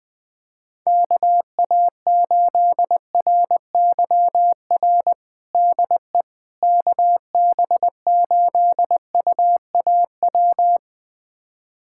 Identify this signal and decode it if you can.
Morse code, heard as a keyed tone, about 20 wpm, 700 Hz. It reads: KA8RYR DE KB8UAW
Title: USA Callsign DE Callsign 20wpm